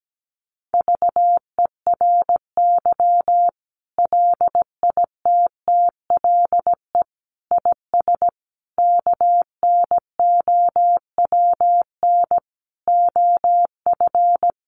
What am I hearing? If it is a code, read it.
Morse code, heard as a keyed tone, about 17 wpm, 700 Hz. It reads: VERY LITTLE IS KNOWN OF
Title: Sentences from Top 1500 Words 17wpm